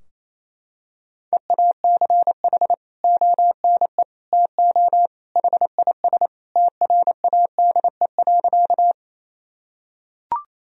Morse code, heard as a keyed tone, about 28 wpm, 700 Hz. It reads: EACH ONE TO HIS TRADE.
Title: Sentences from Top 1000 Words - Review Part 4 - 28wpm